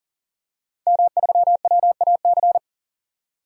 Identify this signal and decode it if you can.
Morse code, heard as a keyed tone, about 40 wpm, 700 Hz. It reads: M3WAC
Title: DXCC-100 Call Signs - 5 Characters - 40wpm